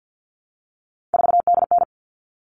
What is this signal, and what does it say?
Morse code, heard as a keyed tone, about 50 wpm, 700 Hz. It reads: 4DN
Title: All Cut Numbers - 3-Digit 50wpm